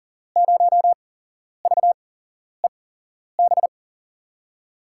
Morse code, heard as a keyed tone, about 39 wpm, spaced as for 11 wpm, 700 Hz. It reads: 0VEB
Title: USA Call Signs - 6 Characters - 8x Character Spacing - 40wpm